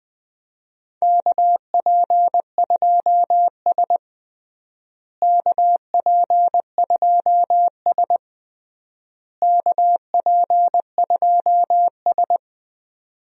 Morse code, heard as a keyed tone, about 20 wpm, 700 Hz. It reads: KP2S KP2S KP2S
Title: DXCC-100 Call Signs - 4 Characters - Repeated 3x - 20wpm